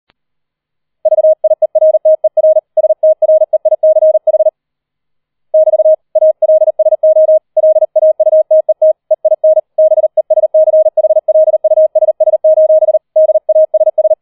Morse code, heard as a keyed tone, about 38 wpm, 630 Hz. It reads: VIERTER STREICH <BT> ALSO LAUTET EIN BESCHLUSS: DASS
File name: 004 Vierter Streich 40wpm